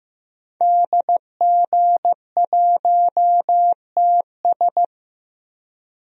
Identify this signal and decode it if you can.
Morse code, heard as a keyed tone, about 15 wpm, 700 Hz. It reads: DG1TS
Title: DXCC-100 Call Signs - 5 Characters - Repeated 2x - 15wpm